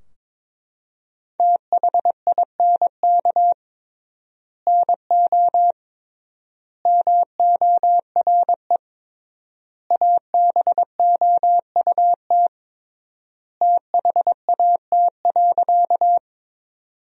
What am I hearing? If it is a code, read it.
Morse code, heard as a keyed tone, about 22 wpm, 700 Hz. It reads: THINK NO MORE ABOUT THAT.
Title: 3x Word-Spacing - Sentences from Top 200 Words - 22wpm